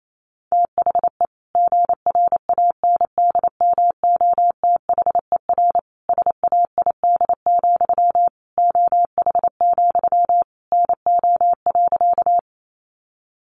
Morse code, heard as a keyed tone, about 28 wpm, 700 Hz. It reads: THE GRANDMOTHER SAID, OH, NO.
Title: Sentences from Top 1200 Words 28wpm